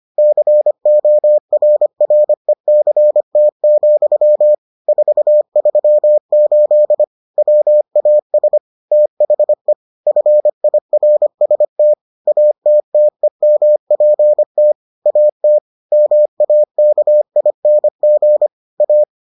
Morse code, heard as a keyed tone, about 25 wpm, 600 Hz. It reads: CORRECT, 438 WAS THE FIRST ATTEMPT AT MAKING A